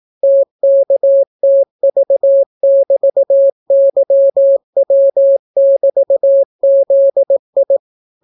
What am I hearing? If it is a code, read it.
Morse code, heard as a keyed tone, about 18 wpm, 550 Hz. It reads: TKTV<BT>YW<BT>ZI